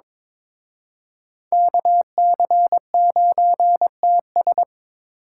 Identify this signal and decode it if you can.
Morse code, heard as a keyed tone, about 26 wpm, 700 Hz. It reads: KC9TS
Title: Speed Racing - USA 2x2 Call Signs 22wpm